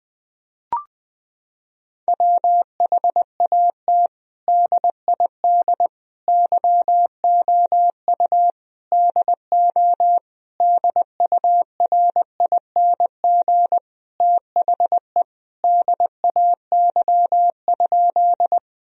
Morse code, heard as a keyed tone, about 21 wpm, 700 Hz. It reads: WHAT DID YOU DO DURING THE DAY?
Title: Sentences from Top 1000 Words - Review Part 2 - 20wpm